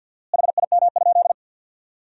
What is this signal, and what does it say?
Morse code, heard as a keyed tone, about 50 wpm, 700 Hz. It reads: HIM?
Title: 4x Word-Spacing - Sentences from Top 100 Words - 50wpm